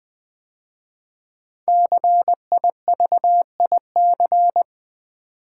Morse code, heard as a keyed tone, about 20 wpm, 700 Hz. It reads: CIVIC
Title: 5 Letter Words - First 3 Letters Given - 20wpm